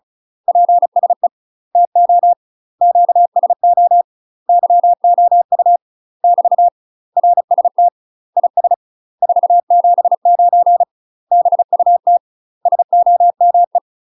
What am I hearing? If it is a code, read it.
Morse code, heard as a keyed tone, about 34 wpm, 700 Hz. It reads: PSE TO QSO YOU <BT> RST IS 479 BUT SOME
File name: UK MASTER QSOs TRAINING / 35WPM / UK MASTER QSOs 0024